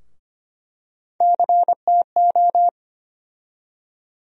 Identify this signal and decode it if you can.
Morse code, heard as a keyed tone, about 25 wpm, 700 Hz. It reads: CTO
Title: USA Call Sign Postfixes - 3 Letters 25wpm